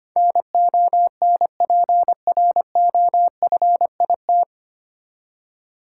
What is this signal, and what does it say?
Morse code, heard as a keyed tone, about 25 wpm, 700 Hz. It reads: NONPROFIT
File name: Nine Letter Words - 25wpm